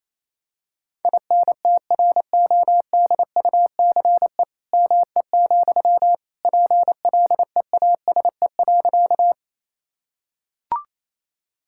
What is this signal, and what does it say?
Morse code, heard as a keyed tone, about 28 wpm, 700 Hz. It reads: INTRODUCE ME, PLEASE.
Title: Sentences from Top 1600 Words 28wpm